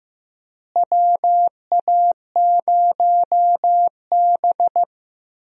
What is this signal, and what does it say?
Morse code, heard as a keyed tone, about 15 wpm, 700 Hz. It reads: WA0B
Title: USA Call Signs - 4 Characters - Repeated 3x - 15wpm